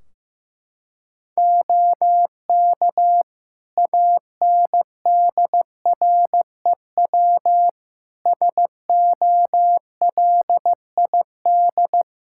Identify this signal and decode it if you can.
Morse code, heard as a keyed tone, about 15 wpm, 700 Hz. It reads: OK ANDREW SOLID
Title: Practice QSO - Exchange 3 - More Info - 15wpm